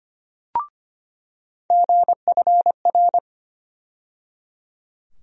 Morse code, heard as a keyed tone, about 25 wpm, 700 Hz. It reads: GFR
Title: USA Call Sign Postfixes - 3 Letters 25wpm